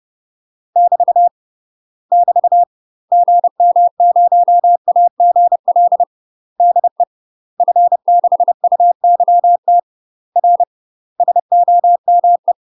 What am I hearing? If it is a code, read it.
Morse code, heard as a keyed tone, about 30 wpm, 700 Hz. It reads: <BT> <BT> GM0AGL DE F6UYT R SOME
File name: UK MASTER QSOs TRAINING / 30WPM / UK MASTER QSOs 0051